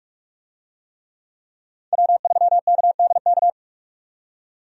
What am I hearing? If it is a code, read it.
Morse code, heard as a keyed tone, about 46 wpm, 700 Hz. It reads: W3KDK
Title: USA Call Signs - 5 Characters - 45wpm